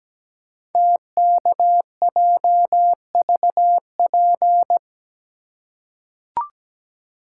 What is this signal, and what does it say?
Morse code, heard as a keyed tone, about 17 wpm, 700 Hz. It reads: TKJVP
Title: Code Groups - 5 Letters - 17wpm